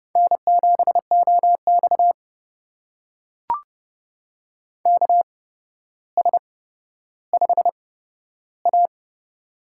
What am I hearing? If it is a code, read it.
Morse code, heard as a keyed tone, about 30 wpm, spaced as for 13 wpm, 700 Hz. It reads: N7OX KS5A
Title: USA Call Signs - 4 Characters - 8x Character Spacing - 30wpm